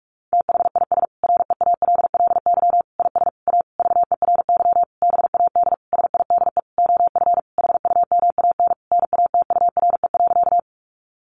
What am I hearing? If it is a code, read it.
Morse code, heard as a keyed tone, about 44 wpm, 700 Hz. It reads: THIS REALLY IS A VERY BAD SIDE OF HUMAN NATURE.
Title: Sentences from Top 700 Words 45wpm